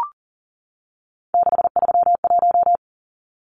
Morse code, heard as a keyed tone, about 40 wpm, 700 Hz. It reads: 631
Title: Numbers - 3-Digit 40wpm